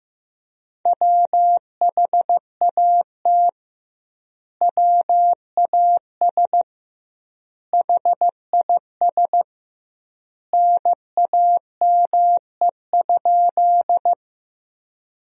Morse code, heard as a keyed tone, about 15 wpm, 700 Hz. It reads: WHAT WAS HIS NAME?
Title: 2x Word-Spacing - Sentences from Top 200 Words - 15wpm